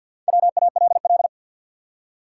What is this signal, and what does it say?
Morse code, heard as a keyed tone, about 53 wpm, 700 Hz. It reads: WALL
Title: Four Letter Words - 50wpm